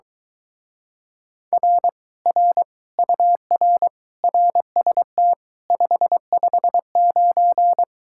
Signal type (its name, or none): Morse code